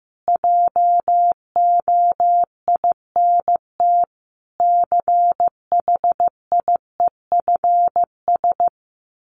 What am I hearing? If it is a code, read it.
Morse code, heard as a keyed tone, about 15 wpm, 700 Hz. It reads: JOINT CHIEFS
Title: Sets of 5 Words Encore 15wpm